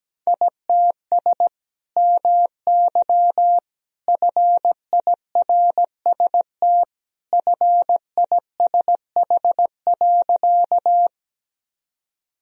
Morse code, heard as a keyed tone, about 17 wpm, 700 Hz. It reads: ITS MY FIRST FISH.
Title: Sentences from Top 1000 Words - Review Part 1 - 17wpm